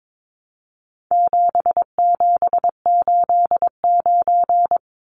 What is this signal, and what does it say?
Morse code, heard as a keyed tone, about 22 wpm, 700 Hz. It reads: 7789
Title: Numbers - 4-Digit 22wpm